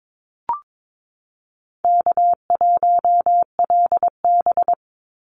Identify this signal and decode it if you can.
Morse code, heard as a keyed tone, about 22 wpm, 700 Hz. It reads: K1LB
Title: Easy Call Signs Repeated 2x - 22wpm